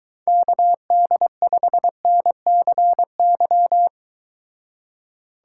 Morse code, heard as a keyed tone, about 22 wpm, 700 Hz. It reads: KD5NCY